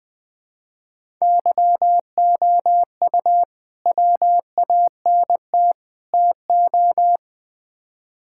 Morse code, heard as a keyed tone, about 20 wpm, 700 Hz. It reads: YOU WANT TO
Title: Build A Phrase - 3-Word Phrases - 20wpm